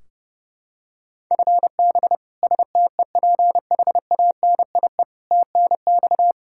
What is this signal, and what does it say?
Morse code, heard as a keyed tone, about 32 wpm, 700 Hz. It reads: FB STEPHANIE TNX
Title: Practice QSO - Exchange 3 - More Info - 30wpm